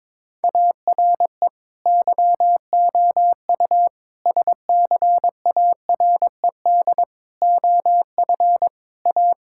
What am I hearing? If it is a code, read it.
Morse code, heard as a keyed tone, about 22 wpm, 700 Hz. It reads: ARE YOU SCARED OF A
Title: Sentences from Top 1000 Words - Review Part 5 - 22wpm